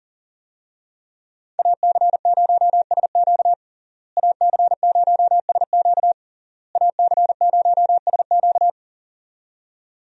Morse code, heard as a keyed tone, about 40 wpm, 690 Hz. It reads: AC0SQ AC0SQ AC0SQ
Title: USA Call Signs - 5 Characters - Repeated 3x - 40wpm